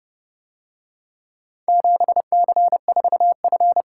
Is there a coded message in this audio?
Yes: Morse code from 1.7 to 3.8 s, heard as a keyed tone, about 27 wpm, 700 Hz. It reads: 7C4F